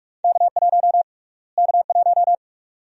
Morse code, heard as a keyed tone, about 48 wpm, 700 Hz. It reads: K1 K1
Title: USA Call Sign Prefixes - Repeated 2x - 45wpm